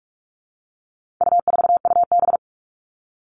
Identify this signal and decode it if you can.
Morse code, heard as a keyed tone, about 44 wpm, 700 Hz. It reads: U4UB
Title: All Cut Numbers - 4-Digit 45wpm